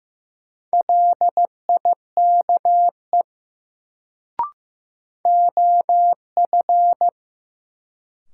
Morse code, heard as a keyed tone, about 15 wpm, 700 Hz. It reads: LIKE OF
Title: Sentences from Top 100 Words - New Word Drill - 15wpm